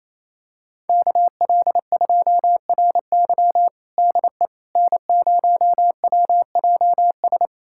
Morse code, heard as a keyed tone, about 28 wpm, 700 Hz. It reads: KL2RY DE N0WJS
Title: USA Callsign DE Callsign - 5 Characters - 28wpm